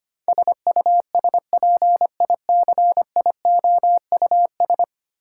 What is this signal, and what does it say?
Morse code, heard as a keyed tone, about 25 wpm, 700 Hz. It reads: SUSPICIOUS
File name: 10 Letter Words - First 3 Letters Given - 25wpm